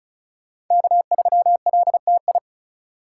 Morse code, heard as a keyed tone, about 34 wpm, 700 Hz. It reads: K3LTI
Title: USA Call Signs - 5 Characters - 4x Character Spacing - 35wpm